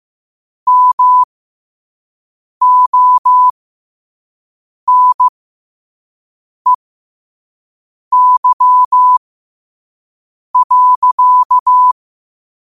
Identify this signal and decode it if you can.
Morse code, heard as a keyed tone, about 15 wpm, spaced as for 5 wpm, 1000 Hz. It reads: MONEY.